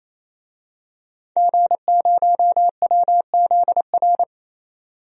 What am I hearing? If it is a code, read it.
Morse code, heard as a keyed tone, about 28 wpm, 700 Hz. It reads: G0WZR
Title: DXCC-100 Call Signs - 5 Characters - 28wpm